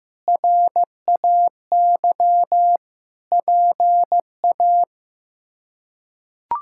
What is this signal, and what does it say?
Morse code, heard as a keyed tone, about 15 wpm, 700 Hz. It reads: RAY PA
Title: K1USN SST QSO Practice - 15wpm